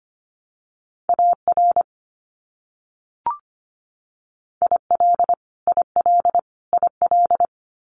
Morse code, heard as a keyed tone, about 25 wpm, 700 Hz. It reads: AR IL IL IL
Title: US States Repeated 3x 25wpm